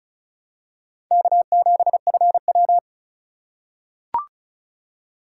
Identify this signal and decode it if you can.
Morse code, heard as a keyed tone, about 36 wpm, 690 Hz. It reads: K7FW
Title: USA Call Signs - 4 Characters - 2x Character Spacing - 35wpm